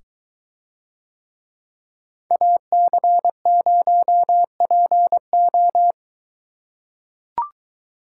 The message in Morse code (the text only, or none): AC0PO